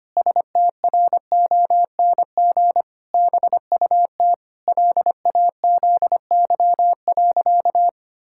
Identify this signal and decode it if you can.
Morse code, heard as a keyed tone, about 25 wpm, 700 Hz. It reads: STRONG BUT LAZY.
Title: Pangram Challenge v2 - 25wpm